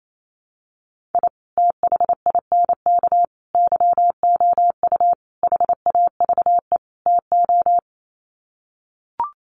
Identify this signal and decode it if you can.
Morse code, heard as a keyed tone, about 28 wpm, 700 Hz. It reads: I THINK YOU HAVE TO
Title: Sets of 5 Words 28wpm